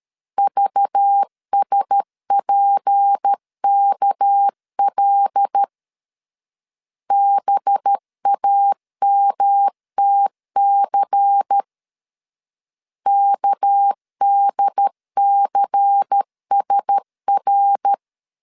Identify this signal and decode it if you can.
Morse code, heard as a keyed tone, about 13 wpm, 800 Hz. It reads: VSPKL BAMTC KDCSR
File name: LEZIONI / 6 PKX0 / 6 PK 12wpm 2006